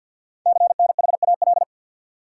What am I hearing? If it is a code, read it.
Morse code, heard as a keyed tone, about 48 wpm, 700 Hz. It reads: CNHAL
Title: Code Groups - 5 Letters - 50wpm